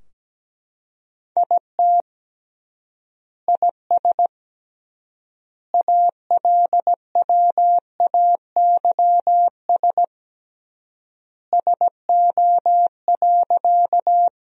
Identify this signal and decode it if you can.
Morse code, heard as a keyed tone, about 17 wpm, 700 Hz. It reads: IT IS ALWAYS SO.
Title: 3x Word-Spacing - Sentences from Top 300 Words - 17wpm